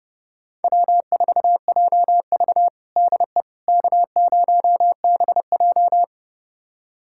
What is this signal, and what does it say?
Morse code, heard as a keyed tone, about 30 wpm, 700 Hz. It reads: W4JV DE K0BJ
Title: USA Callsign DE Callsign - 4 Characters - 30wpm